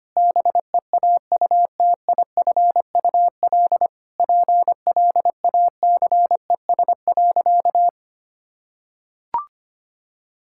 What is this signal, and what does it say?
Morse code, heard as a keyed tone, about 25 wpm, 700 Hz. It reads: BEAUTIFUL PLACES.
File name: practice / 25wpm / Sentences with 4-Word Phrases - 25wpm